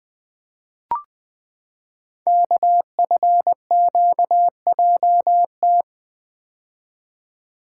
Morse code, heard as a keyed tone, about 20 wpm, 700 Hz. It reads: KFQJT
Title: Code Groups - 5 Letters - 20wpm